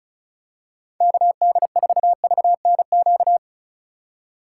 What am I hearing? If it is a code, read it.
Morse code, heard as a keyed tone, about 34 wpm, 700 Hz. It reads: KD4VNQ
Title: USA Call Signs - 6 Characters - 2x Character Spacing - 35wpm